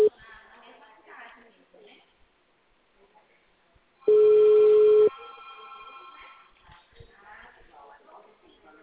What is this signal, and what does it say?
Ringback tone (Europe, 425 Hz), with speech.
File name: out- -2044- - - .2955